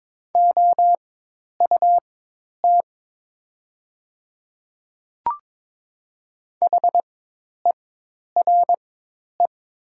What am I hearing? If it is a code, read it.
Morse code, heard as a keyed tone, about 22 wpm, spaced as for 10 wpm, 700 Hz. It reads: OUT HERE
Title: 4x Character Spacing - Top 300 Words - 22wpm